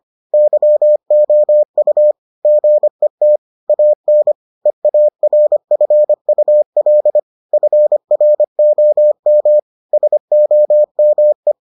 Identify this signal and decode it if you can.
Morse code, heard as a keyed tone, about 25 wpm, 600 Hz. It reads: YOU GET AN EARFUL FROM SOME